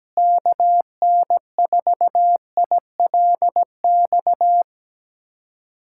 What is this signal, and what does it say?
Morse code, heard as a keyed tone, about 17 wpm, 700 Hz. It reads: KN4ILX
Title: USA Callsign DE Callsign - 6 Characters - 17wpm